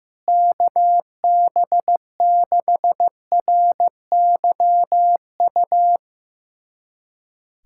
Morse code, heard as a keyed tone, about 15 wpm, 700 Hz. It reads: KB6RYU
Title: USA Callsign DE Callsign - 6 Characters - 15wpm